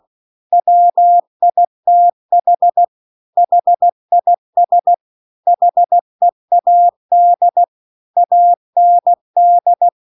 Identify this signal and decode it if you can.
Morse code, heard as a keyed tone, about 16 wpm, 700 Hz. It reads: WITH HIS HEAD AND
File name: CW-The-Big-Twoheaded-River-2-16WPM